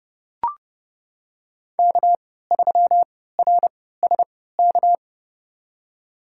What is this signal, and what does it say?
Morse code, heard as a keyed tone, about 30 wpm, spaced as for 17 wpm, 700 Hz. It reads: K3RSK
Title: USA Call Signs - 5 Characters - 3x Character Spacing - 30wpm